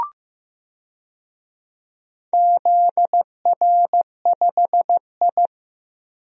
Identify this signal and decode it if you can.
Morse code, heard as a keyed tone, about 15 wpm, 700 Hz. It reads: ZR5I
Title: DXCC-100 Call Signs - 4 Characters - Repeated 2x - 15wpm